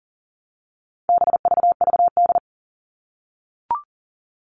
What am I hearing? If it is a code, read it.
Morse code, heard as a keyed tone, about 40 wpm, 700 Hz. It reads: BVVD
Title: All Cut Numbers - 4-Digit 40wpm